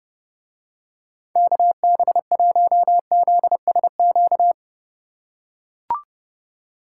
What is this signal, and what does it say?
Morse code, heard as a keyed tone, about 30 wpm, 700 Hz. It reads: KB1ZSQ